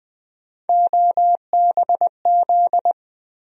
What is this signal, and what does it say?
Morse code, heard as a keyed tone, about 20 wpm, 700 Hz. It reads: OBZ
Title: USA Call Sign Postfixes - 3 Letters 20wpm